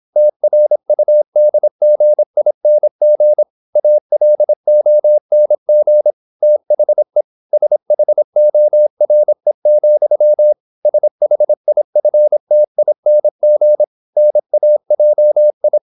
Morse code, heard as a keyed tone, about 26 wpm, 600 Hz. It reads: TRUDGING ALONG THE SHORE, SHIFTING NAJI
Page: Never 26wpm | CW med Gnister